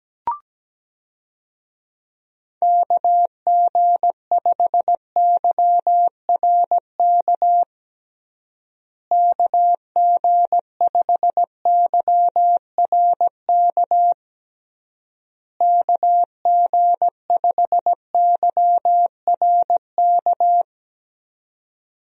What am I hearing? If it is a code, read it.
Morse code, heard as a keyed tone, about 17 wpm, 700 Hz. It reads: KG5YRK KG5YRK KG5YRK
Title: USA Call Signs - 6 Characters - Repeated 3x - 17wpm